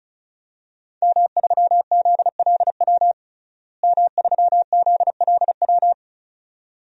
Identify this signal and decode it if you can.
Morse code, heard as a keyed tone, about 34 wpm, 700 Hz. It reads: M3ZLW M3ZLW
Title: DXCC-100 Call Signs - 5 Characters - Repeated 2x - 35wpm